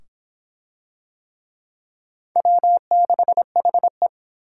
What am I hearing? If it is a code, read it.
Morse code, heard as a keyed tone, about 21 wpm, 700 Hz. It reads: W6HE
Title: Speed Racing - USA 1x2 Call Signs 17wpm